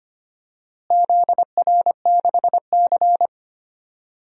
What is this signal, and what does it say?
Morse code, heard as a keyed tone, about 25 wpm, 700 Hz. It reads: ZR6C
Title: DXCC-100 Call Signs - 4 Characters - Repeated 3x - 25wpm